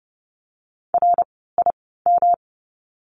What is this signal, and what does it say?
Morse code, heard as a keyed tone, about 30 wpm, spaced as for 17 wpm, 700 Hz. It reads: RIM
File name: course / 30:10wpm / 108 Lesson 18f - Letter m - Words 30wpm